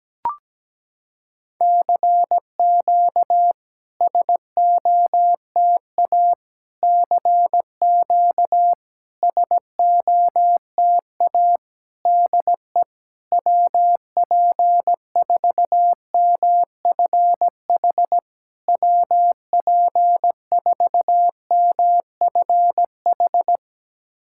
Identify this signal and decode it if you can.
Morse code, heard as a keyed tone, about 17 wpm, 700 Hz. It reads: CQ SOTA CQ SOTA DE WP4MFH WP4MFH
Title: SOTA QSO Practice 17wpm